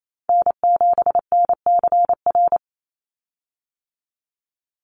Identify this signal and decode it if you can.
Morse code, heard as a keyed tone, about 28 wpm, 700 Hz. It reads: N7NCR
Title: USA Call Signs 28wpm